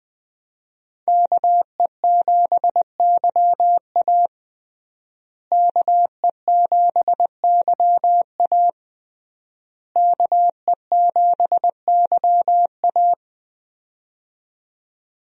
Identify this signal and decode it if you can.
Morse code, heard as a keyed tone, about 20 wpm, 700 Hz. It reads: KE7YA KE7YA KE7YA
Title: USA Call Signs - 5 Characters - Repeated 3x - 20wpm